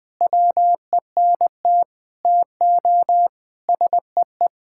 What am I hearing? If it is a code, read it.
Morse code, heard as a keyed tone, about 20 wpm, 700 Hz. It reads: WENT TO SEE
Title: Sentences from Top 1000 Words - Review Part 5 - 20wpm